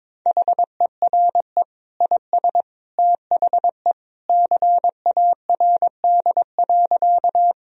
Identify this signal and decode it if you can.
Morse code, heard as a keyed tone, about 22 wpm, 700 Hz. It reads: HERE IS THE CARD.
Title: Sentences from Top 1000 Words - Review Part 4 - 22wpm